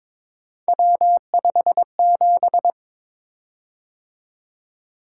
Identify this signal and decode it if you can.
Morse code, heard as a keyed tone, about 22 wpm, 700 Hz. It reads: W57
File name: practice / 22wpm / Three Letter-Number Combinations - 22wpm